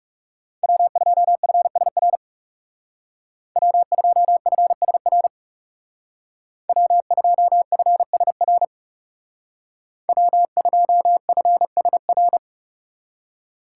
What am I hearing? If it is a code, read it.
Morse code, heard as a keyed tone, about 38 wpm, 690 Hz. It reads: W2FSR W2FSR W2FSR W2FSR